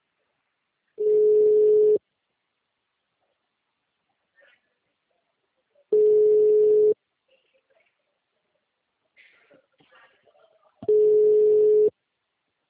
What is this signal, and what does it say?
Ringback tone (Europe, 425 Hz).